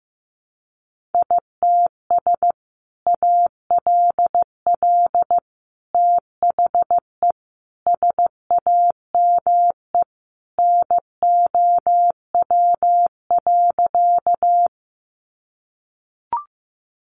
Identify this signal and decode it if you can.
Morse code, heard as a keyed tone, about 15 wpm, 700 Hz. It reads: ITS ALL THE SAME NOW.
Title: Sentences from Top 600 Words 15wpm